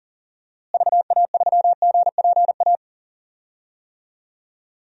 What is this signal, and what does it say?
Morse code, heard as a keyed tone, about 40 wpm, 700 Hz. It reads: VA3GPA
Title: DXCC-100 Call Signs - 6 Characters - Repeated 2x - 40wpm